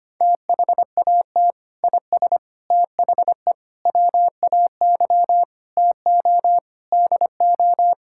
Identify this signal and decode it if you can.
Morse code, heard as a keyed tone, about 25 wpm, 700 Hz. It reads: THAT IS THE WAY TO DO
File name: practice / 25wpm / Sentences with 3-Word Phrases - 25wpm